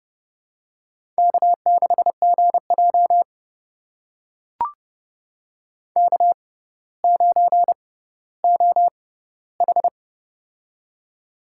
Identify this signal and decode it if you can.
Morse code, heard as a keyed tone, about 29 wpm, spaced as for 15 wpm, 700 Hz. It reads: K6GJ K9OH
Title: USA Call Signs - 4 Characters - 6x Character Spacing - 30wpm